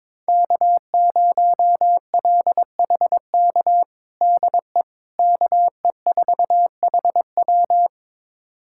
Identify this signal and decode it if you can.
Morse code, heard as a keyed tone, about 21 wpm, 700 Hz. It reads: K0LHK DE KE4HW
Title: USA Callsign DE Callsign - 5 Characters - 22wpm